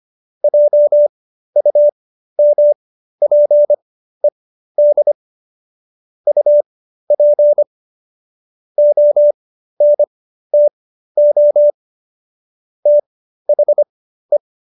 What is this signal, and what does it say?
Morse code, heard as a keyed tone, about 25 wpm, spaced as for 13 wpm, 590 Hz. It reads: JUMPED UP ONTO THE